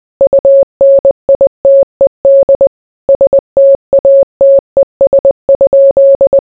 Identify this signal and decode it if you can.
Morse code, heard as a keyed tone, about 20 wpm, 550 Hz. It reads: UNITED STATES?